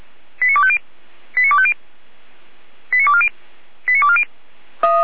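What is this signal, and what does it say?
Five-tone selective call (ZVEI), heard as quick runs of five tones, tones of about 70 ms. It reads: 89250 89250 89240 89240
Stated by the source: Five-tone ZVEI selective call, code 89401, received by SDR.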